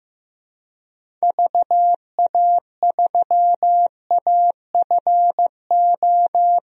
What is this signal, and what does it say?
Morse code, heard as a keyed tone, about 15 wpm, 700 Hz. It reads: VA3AFO
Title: DXCC-100 Call Signs - 6 Characters - 15wpm